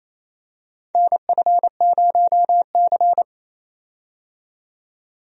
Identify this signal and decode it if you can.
Morse code, heard as a keyed tone, about 28 wpm, 690 Hz. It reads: NF0C
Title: USA Call Signs - 4 Characters - 28wpm